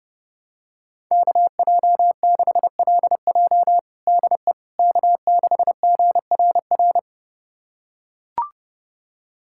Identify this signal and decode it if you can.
Morse code, heard as a keyed tone, about 32 wpm, 700 Hz. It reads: KJ6LJ DE K6GRR
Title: USA Callsign DE Callsign - 5 Characters - 30wpm